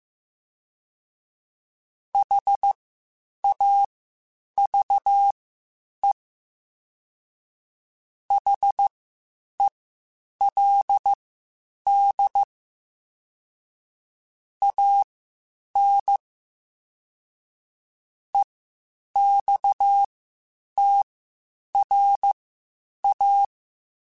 Morse code, heard as a keyed tone, about 15 wpm, spaced as for 8 wpm, 800 Hz. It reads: HAVE HELD AN EXTRA
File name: (spaced) / QSOs Spaced-mp3s / kc7haa @13 sp